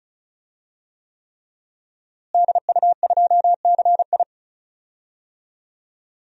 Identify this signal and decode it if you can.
Morse code, heard as a keyed tone, about 36 wpm, 700 Hz. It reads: DU2CI
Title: DXCC-100 Call Signs - 5 Characters - Repeated 3x - 35wpm